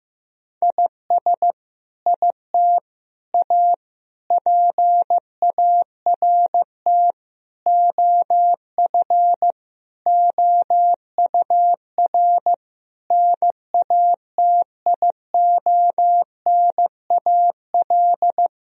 Morse code, heard as a keyed tone, about 15 wpm, 700 Hz. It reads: IS IT A PART OF OUR NATIONAL
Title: Sentences from Top 1000 Words - Review Part 4 - 15wpm